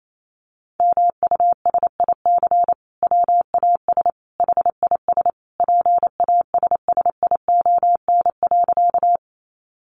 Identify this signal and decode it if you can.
Morse code, heard as a keyed tone, about 28 wpm, 700 Hz. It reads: MUSIC WAS HIS PASSION.
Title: Sentences from Top 1300 Words 28wpm